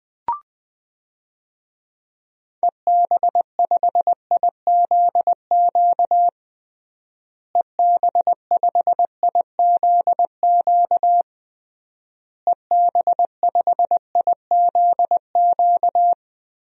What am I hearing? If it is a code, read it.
Morse code, heard as a keyed tone, about 20 wpm, 700 Hz. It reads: EB5IZQ EB5IZQ EB5IZQ
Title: DXCC-100 Call Signs - 6 Characters - Repeated 3x - 20wpm